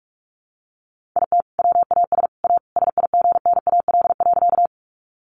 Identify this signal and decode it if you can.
Morse code, heard as a keyed tone, about 44 wpm, 700 Hz. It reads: IT WAS A SIGNAL.
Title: Sentences from Top 1400 Words 45wpm